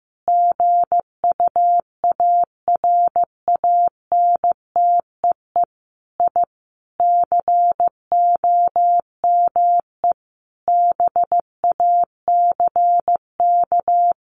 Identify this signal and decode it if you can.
Morse code, heard as a keyed tone, about 15 wpm, 700 Hz. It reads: GUARANTEE I COME BACK
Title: A Sound Of Thunder 15wpm